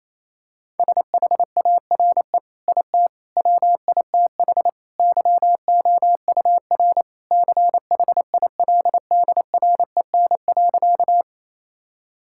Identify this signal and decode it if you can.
Morse code, heard as a keyed tone, about 28 wpm, 700 Hz. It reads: SHARE IT WITH YOUR CHILDREN.
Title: Sentences from Top 1000 Words - Review Part 6 - 28wpm